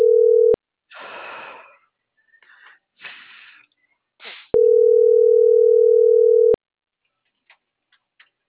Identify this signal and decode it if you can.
Ringback tone (North America), 440 + 480 Hz.